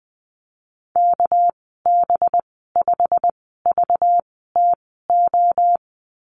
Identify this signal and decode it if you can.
Morse code, heard as a keyed tone, about 20 wpm, spaced as for 15 wpm, 700 Hz. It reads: KB5VTO
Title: 219 Lesson 35g - Number 0 - Call Signs 20wpm